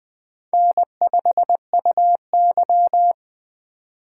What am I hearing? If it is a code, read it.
Morse code, heard as a keyed tone, about 20 wpm, 700 Hz. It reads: N5UY
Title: USA Call Signs - 4 Characters - 2x Character Spacing - 20wpm